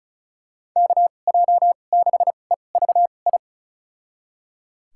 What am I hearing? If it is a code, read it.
Morse code, heard as a keyed tone, about 36 wpm, spaced as for 25 wpm, 700 Hz. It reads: KJ6EVI
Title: USA Call Signs - 2x Character Spacing - 35wpm